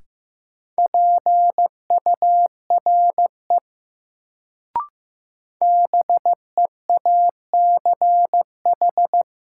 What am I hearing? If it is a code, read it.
Morse code, heard as a keyed tone, about 15 wpm, 700 Hz. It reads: PURE BEACH
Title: Sentences from Top 1500 Words - New Word Drill - 15wpm